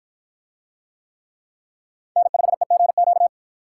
Morse code, heard as a keyed tone, about 50 wpm, 700 Hz. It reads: N5EGQ